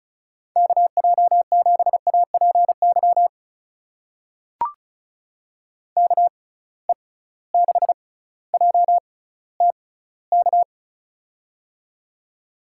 Morse code, heard as a keyed tone, about 34 wpm, spaced as for 18 wpm, 700 Hz. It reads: KJ7APY KE6JTK
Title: USA Call Signs - 6 Characters - 6x Character Spacing - 35wpm